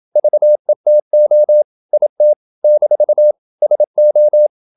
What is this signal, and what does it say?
Morse code, heard as a keyed tone, about 27 wpm, 600 Hz. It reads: VETO IT - SO
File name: Never 0027wpm